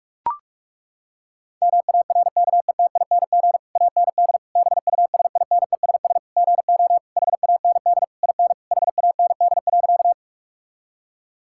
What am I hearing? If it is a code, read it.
Morse code, heard as a keyed tone, about 41 wpm, 700 Hz. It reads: MARKETING AND BUSINESS GO HAND IN HAND.
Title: Sentences from Top 1000 Words - Review Part 5 - 45wpm